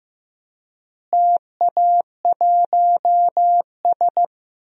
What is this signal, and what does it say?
Morse code, heard as a keyed tone, about 15 wpm, 700 Hz. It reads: TA1S
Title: DXCC-100 Call Signs - 4 Characters - Repeated 3x - 15wpm